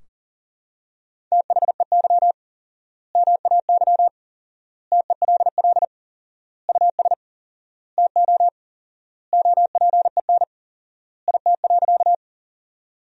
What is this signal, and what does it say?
Morse code, heard as a keyed tone, about 37 wpm, 700 Hz. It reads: THEY MAY TELL US TO OPEN IT.
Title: 4x Word-Spacing - Sentences from Top 300 Words - 40wpm